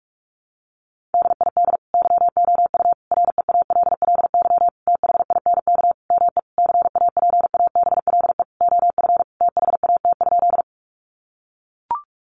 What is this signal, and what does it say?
Morse code, heard as a keyed tone, about 44 wpm, 700 Hz. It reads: DID YOU REALLY THINK ME CAPABLE OF THAT?
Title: Sentences from Top 1200 Words 45wpm